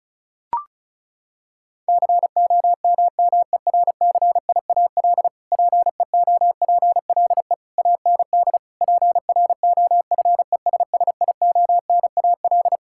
Morse code, heard as a keyed tone, about 36 wpm, 700 Hz. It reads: COMMERCIAL PEOPLE AND PROFESSIONAL
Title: Sentences from Top 1000 Words - Review Part 5 - 35wpm